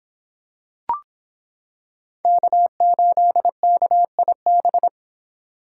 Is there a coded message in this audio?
Yes: Morse code from 2.2 to 4.9 s, heard as a keyed tone, about 26 wpm, 700 Hz. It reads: K8KIB